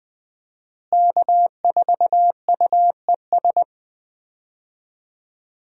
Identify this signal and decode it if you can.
Morse code, heard as a keyed tone, about 20 wpm, 700 Hz. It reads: K4UES
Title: USA Call Signs - 5 Characters - Repeated 3x - 20wpm